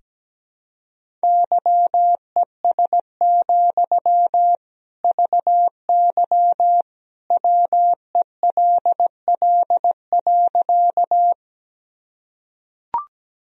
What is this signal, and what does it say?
Morse code, heard as a keyed tone, about 17 wpm, 700 Hz. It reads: YES, VY WELL.
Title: Sentences from Top 500 Words and Abbreviations 17wpm